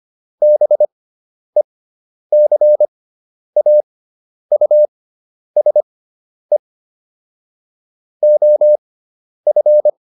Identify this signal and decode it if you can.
Morse code, heard as a keyed tone, about 25 wpm, spaced as for 10 wpm, 600 Hz. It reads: BECAUSE OF